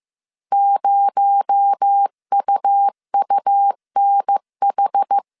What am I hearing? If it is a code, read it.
Morse code, heard as a keyed tone, about 15 wpm, 800 Hz. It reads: 0UUNH
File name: CW / DET / 1 6 1500C S C 14wpm 2951